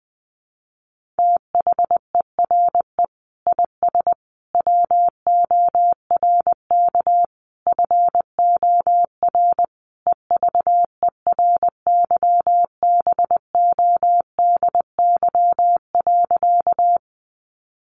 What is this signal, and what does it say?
Morse code, heard as a keyed tone, about 20 wpm, 700 Hz. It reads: THERE IS WORK FOR EVERYBODY.
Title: Sentences from Top 1400 Words 20wpm